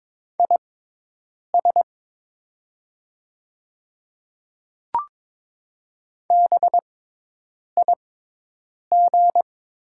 Morse code, heard as a keyed tone, about 22 wpm, spaced as for 8 wpm, 700 Hz. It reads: IS BIG
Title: 6x Character Spacing - Top 200 Words - 22wpm